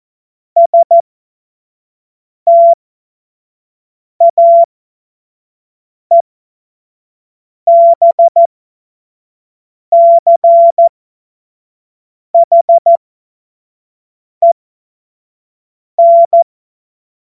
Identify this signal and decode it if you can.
Morse code, heard as a keyed tone, about 14 wpm, spaced as for 5 wpm, 680 Hz. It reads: STAEBCHEN
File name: Morsekurs / Witze / Witz 05. Oktober 2021, 25 F70